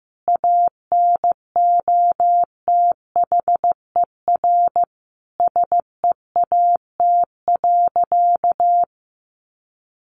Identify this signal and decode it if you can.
Morse code, heard as a keyed tone, about 15 wpm, 700 Hz. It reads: ANOTHER SEAT.
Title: Sentences from Top 1500 Words 15wpm